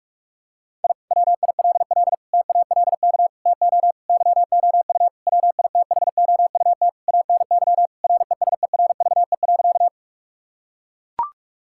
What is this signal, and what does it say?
Morse code, heard as a keyed tone, about 41 wpm, 700 Hz. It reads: I WILL TALK TO YOU WITHOUT ANY RESERVE.
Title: Sentences from Top 1000 Words - Review Part 6 - 45wpm